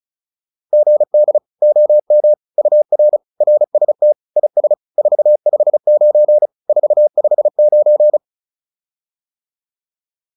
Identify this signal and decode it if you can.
Morse code, heard as a keyed tone, about 36 wpm, 600 Hz. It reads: GD OM UR RST IS 459 459
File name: QSO406 35